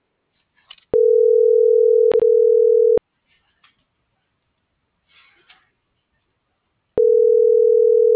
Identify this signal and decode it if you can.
Ringback tone (North America), 440 + 480 Hz.